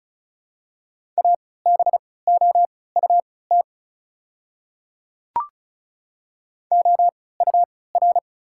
Morse code, heard as a keyed tone, about 36 wpm, spaced as for 20 wpm, 700 Hz. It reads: ABOUT OUR
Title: 3x Character Spacing - Top 300 Words - 35wpm